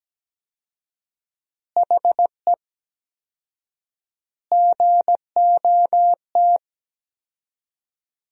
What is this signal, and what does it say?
Morse code, heard as a keyed tone, about 17 wpm, 700 Hz. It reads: HE GOT
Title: 4x Word-Spacing - Sentences from Top 300 Words - 17wpm